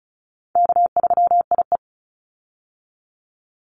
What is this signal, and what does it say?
Morse code, heard as a keyed tone, about 36 wpm, 700 Hz. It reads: K3IE
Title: Call Signs - 35wpm